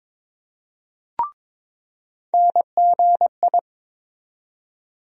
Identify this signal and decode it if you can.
Morse code, heard as a keyed tone, about 22 wpm, 700 Hz. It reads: NGI
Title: USA Call Sign Postfixes - 3 Letters 22wpm